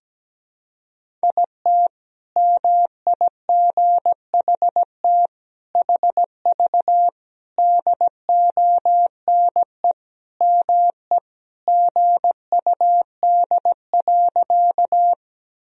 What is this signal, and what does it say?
Morse code, heard as a keyed tone, about 17 wpm, 700 Hz. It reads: IT MIGHT HV DONE ME GUD.
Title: Sentences from Top 500 Words and Abbreviations 17wpm